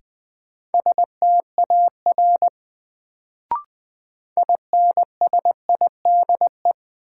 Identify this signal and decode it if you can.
Morse code, heard as a keyed tone, about 20 wpm, 700 Hz. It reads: STAR INSIDE
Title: Sentences from Top 1000 Words - Review Part 2 - New Word Drill - 20wpm